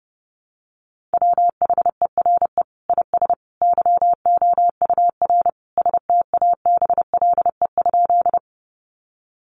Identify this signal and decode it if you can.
Morse code, heard as a keyed tone, about 30 wpm, 700 Hz. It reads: WHERE IS YOUR STABLE?
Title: Sentences from Top 1500 Words 30wpm